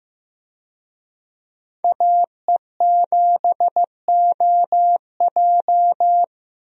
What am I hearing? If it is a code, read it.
Morse code, heard as a keyed tone, about 16 wpm, 700 Hz. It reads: AE7OJ
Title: USA Call Signs - 5 Characters - Repeated 3x - 15wpm